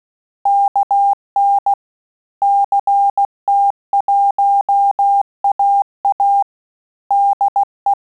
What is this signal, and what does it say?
Morse code, heard as a keyed tone, about 16 wpm, 800 Hz. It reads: KN CT1AA DE
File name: QSO CW 75 CPM